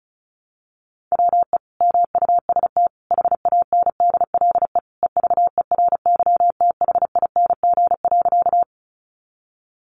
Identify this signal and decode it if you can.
Morse code, heard as a keyed tone, about 34 wpm, 700 Hz. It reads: WE MUST HANDLE EVERYTHING.
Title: Sentences from Top 1100 Words 35wpm